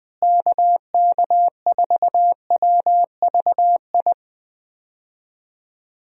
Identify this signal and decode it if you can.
Morse code, heard as a keyed tone, about 19 wpm, 700 Hz. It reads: KK4WVI
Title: USA Call Signs - 6 Characters - Repeated 3x - 20wpm